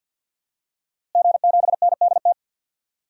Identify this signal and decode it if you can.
Morse code, heard as a keyed tone, about 52 wpm, 700 Hz. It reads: G7NDT
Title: DXCC-100 Call Signs - 5 Characters - Repeated 2x - 50wpm